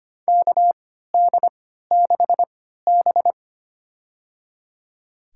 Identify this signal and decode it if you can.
Morse code, heard as a keyed tone, about 25 wpm, spaced as for 14 wpm, 700 Hz. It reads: KD6B
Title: USA Call Signs - 4 Characters - 3x Character Spacing - 25wpm